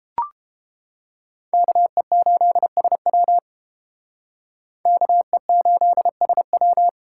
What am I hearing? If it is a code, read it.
Morse code, heard as a keyed tone, about 30 wpm, 700 Hz. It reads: KE8SW KE8SW